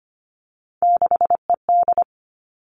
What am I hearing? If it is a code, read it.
Morse code, heard as a keyed tone, about 25 wpm, 700 Hz. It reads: 6ED
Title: Three Challenging Letter-Number Combinations 25wpm